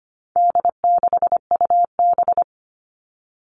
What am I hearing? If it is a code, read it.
Morse code, heard as a keyed tone, about 25 wpm, 700 Hz. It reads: D6UB
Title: All Cut Numbers - 4-Digit 25wpm